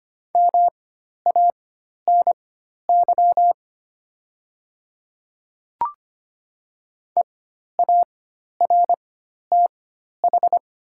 Morse code, heard as a keyed tone, about 25 wpm, spaced as for 12 wpm, 700 Hz. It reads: MANY EARTH
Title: 4x Character Spacing - Top 200 Words - 25wpm